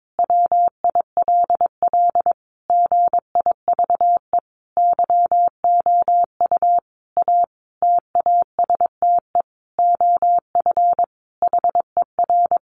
Morse code, heard as a keyed tone, about 22 wpm, 700 Hz. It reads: WILL GIVE YOU A TASTE OF HER
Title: Sentences from Top 1100 Words 22wpm